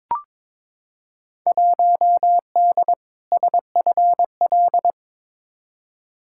Morse code, heard as a keyed tone, about 23 wpm, 700 Hz. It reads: 1D SFL
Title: Field Day Exchanges - 22wpm